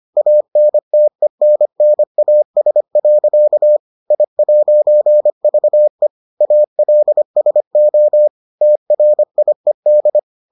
Morse code, heard as a keyed tone, about 25 wpm, 600 Hz. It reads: ANTENNAS. I'VE ALSO TRIED